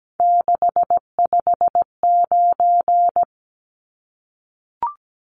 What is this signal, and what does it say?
Morse code, heard as a keyed tone, about 17 wpm, 700 Hz. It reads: 659
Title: Numbers - 3-Digit 17wpm